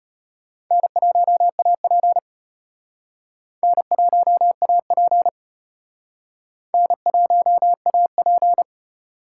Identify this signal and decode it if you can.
Morse code, heard as a keyed tone, about 32 wpm, 700 Hz. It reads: N1AP N1AP N1AP